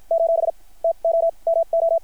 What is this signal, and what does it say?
Morse code, heard as a keyed tone, about 53 wpm, spaced as for 39 wpm, 690 Hz. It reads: : TOMO
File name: archived / CRSnet